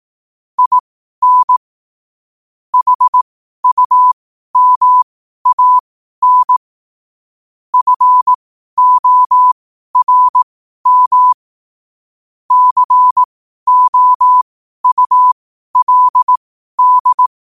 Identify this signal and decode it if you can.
Morse code, heard as a keyed tone, about 18 wpm, spaced as for 12 wpm, 1000 Hz. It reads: IN HUMAN FORM COULD
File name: qotdfiles / 12wpm